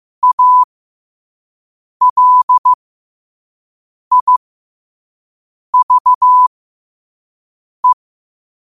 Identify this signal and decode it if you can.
Morse code, heard as a keyed tone, about 15 wpm, spaced as for 5 wpm, 1000 Hz. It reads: ALIVE
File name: qotdfiles / 5wpm